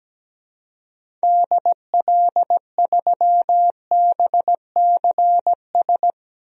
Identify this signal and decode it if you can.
Morse code, heard as a keyed tone, about 17 wpm, 700 Hz. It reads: DL3BCS
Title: DXCC-100 Call Signs - 6 Characters - 17wpm